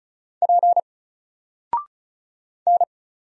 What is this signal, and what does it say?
Morse code, heard as a keyed tone, about 36 wpm, 700 Hz. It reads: P N
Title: Single Character (Letter:Number:Punctuation:Prosign) 35wpm